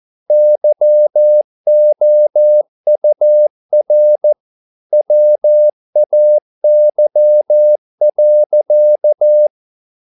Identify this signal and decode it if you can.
Morse code, heard as a keyed tone, about 14 wpm, 600 Hz. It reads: YOUR WAY.
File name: mp3 files / a dogs mistake - 14wpm